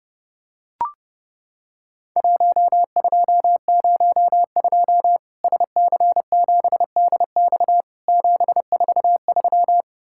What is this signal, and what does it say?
Morse code, heard as a keyed tone, about 30 wpm, 700 Hz. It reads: 1202 SC7DX 743
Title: Worked All Europe DX Contest - QTC Exchange - 30wpm